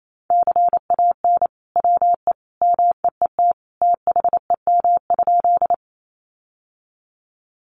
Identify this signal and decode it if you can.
Morse code, heard as a keyed tone, about 28 wpm, 700 Hz. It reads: CAN WE MEET THEM?
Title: Sentences from Top 1100 Words 28wpm